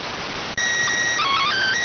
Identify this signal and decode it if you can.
SSTV (slow-scan television), PD 290.